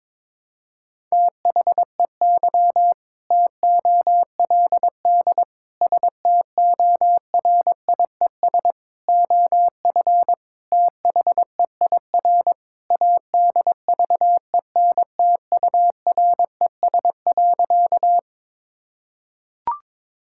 Morse code, heard as a keyed tone, about 22 wpm, 700 Hz. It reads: THEY TOLD STORIES OF THEIR ADVENTURES.
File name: practice / 22wpm / Sentences with 2-Word Phrases - 22wpm